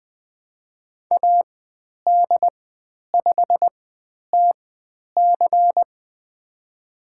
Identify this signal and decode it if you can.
Morse code, heard as a keyed tone, about 20 wpm, spaced as for 10 wpm, 700 Hz. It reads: AD5TC
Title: 095 Lesson 16g - Letter c - Call Signs 20wpm